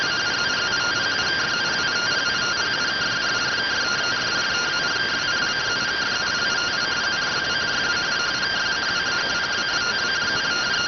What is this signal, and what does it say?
Two-tone teleprinter signal, about 170 Hz shift.